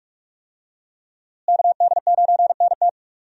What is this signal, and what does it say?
Morse code, heard as a keyed tone, about 45 wpm, 700 Hz. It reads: KD9NT
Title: USA Call Signs - 5 Characters - Repeated 2x - 45wpm